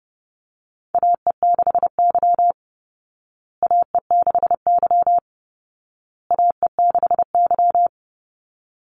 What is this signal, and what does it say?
Morse code, heard as a keyed tone, about 30 wpm, 700 Hz. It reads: AE6Y AE6Y AE6Y
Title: Call Signs Repeated 2x - 30wpm